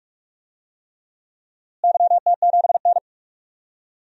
Morse code, heard as a keyed tone, about 46 wpm, 700 Hz. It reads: YT7N
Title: DXCC-100 Call Signs - 4 Characters - Repeated 3x - 45wpm